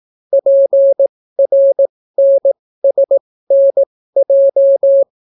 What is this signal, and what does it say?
Morse code, heard as a keyed tone, about 18 wpm, spaced as for 14 wpm, 550 Hz. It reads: PRNSNJ